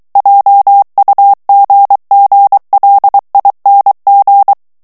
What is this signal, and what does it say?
Morse code, heard as a keyed tone, about 23 wpm, 800 Hz. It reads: JUGGLING
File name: morse mp3 / ANGLAIS texte clair 1 vitesse 22 mots minute